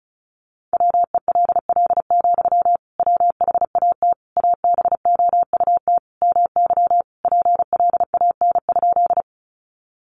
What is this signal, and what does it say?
Morse code, heard as a keyed tone, about 34 wpm, 700 Hz. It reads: WELL, WHAT ABOUT MY PLAN?
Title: Sentences from Top 500 Words 35wpm